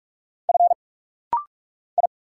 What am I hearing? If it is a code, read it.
Morse code, heard as a keyed tone, about 46 wpm, 700 Hz. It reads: F I
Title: Single Letters - Rapid-Fire 45wpm